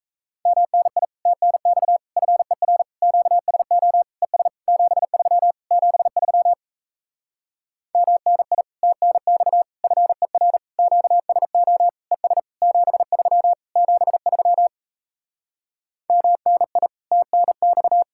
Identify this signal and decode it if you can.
Morse code, heard as a keyed tone, about 37 wpm, 700 Hz. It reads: MNI TNX FER QSO ES 73 73 MNI TNX FER QSO ES 73 73 MNI TNX